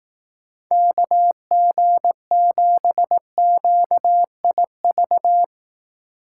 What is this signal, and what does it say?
Morse code, heard as a keyed tone, about 18 wpm, 700 Hz. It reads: KG7QIV
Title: Speed Racing - USA 2x3 Call Signs 15wpm